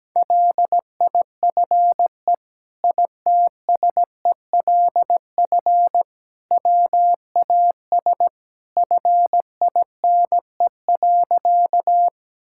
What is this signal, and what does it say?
Morse code, heard as a keyed tone, about 17 wpm, 700 Hz. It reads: LIFE ITSELF WAS FINE.
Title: Sentences from Top 1000 Words - Review Part 5 - 17wpm